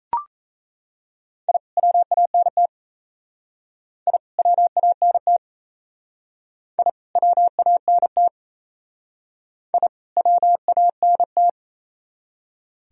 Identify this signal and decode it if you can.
Morse code, heard as a keyed tone, about 36 wpm, 690 Hz. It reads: I WANT I WANT I WANT I WANT